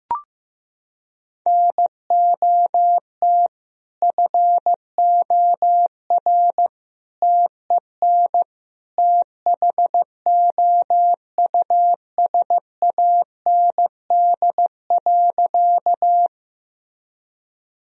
Morse code, heard as a keyed tone, about 15 wpm, 700 Hz. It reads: NOT FOR TEN THOUSAND.
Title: Sentences from Top 1000 Words - Review Part 3 - 15wpm